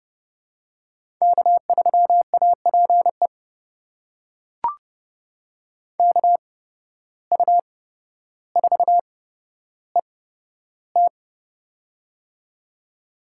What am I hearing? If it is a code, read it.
Morse code, heard as a keyed tone, about 30 wpm, spaced as for 13 wpm, 700 Hz. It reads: K3APE KU4ET
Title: USA Call Signs - 5 Characters - 8x Character Spacing - 30wpm